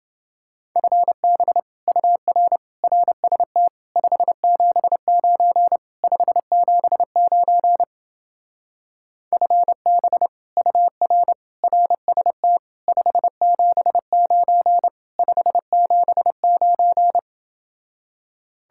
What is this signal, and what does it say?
Morse code, heard as a keyed tone, about 29 wpm, 700 Hz. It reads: FB UR RST 579 579 FB UR RST 579 579